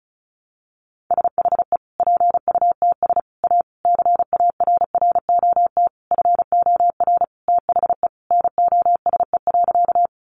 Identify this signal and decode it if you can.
Morse code, heard as a keyed tone, about 34 wpm, 700 Hz. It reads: SHE PUTS A CARROT FOR THE NOSE.
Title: Easy Phrases 35wpm